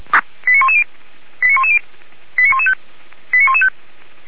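Five-tone selective call (ZVEI), heard as quick runs of five tones, tones of about 70 ms. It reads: 89110 89110 89115 89115
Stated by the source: ZVEI five-tone selective call received on SDR.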